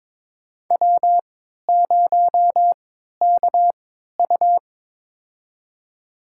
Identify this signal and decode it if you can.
Morse code, heard as a keyed tone, about 22 wpm, spaced as for 13 wpm, 700 Hz. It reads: W0KU
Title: USA Call Signs - 4 Characters - 3x Character Spacing - 22wpm